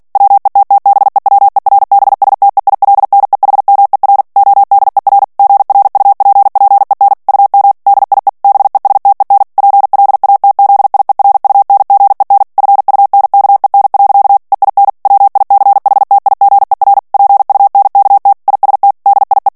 Morse code, heard as a keyed tone, about 48 wpm, 800 Hz. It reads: WETTBEWERBSTEILNEHMER ODER GRUPPEN UM DIE BESTEN PLATZIERUNGEN FUNKEN. EIN WICHTIGER PUNKT IST DIE
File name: morse mp3 / ALLEMAND texte clair 2 vitesse 45 mots minute